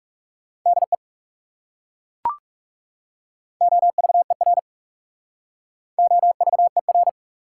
Morse code, heard as a keyed tone, about 40 wpm, 690 Hz. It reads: DE OVER OVER